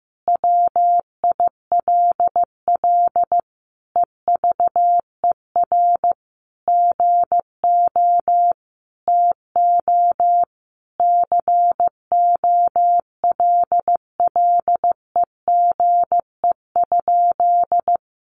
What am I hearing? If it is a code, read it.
Morse code, heard as a keyed tone, about 15 wpm, 700 Hz. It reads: WILL EVER GO TO COLLEGE?
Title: Sentences from Top 1300 Words 15wpm